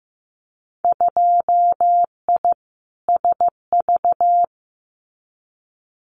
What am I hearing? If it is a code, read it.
Morse code, heard as a keyed tone, about 15 wpm, 700 Hz. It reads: 2I SV
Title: Winter Field Day Exchanges Encore 15wpm